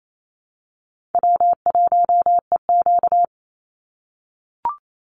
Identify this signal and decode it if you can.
Morse code, heard as a keyed tone, about 28 wpm, 700 Hz. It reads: W1EQ
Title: Call Signs - 28wpm